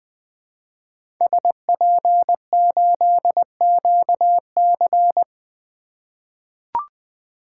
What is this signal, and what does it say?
Morse code, heard as a keyed tone, about 21 wpm, 700 Hz. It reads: SP8QC
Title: 2023 POTA Hunter Call Signs - 20wpm